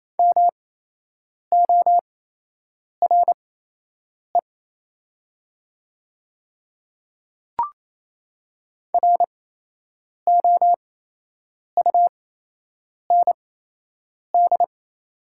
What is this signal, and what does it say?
Morse code, heard as a keyed tone, about 27 wpm, spaced as for 8 wpm, 690 Hz. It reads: MORE ROUND
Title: 8x Character Spacing - Top 200 Words - 28wpm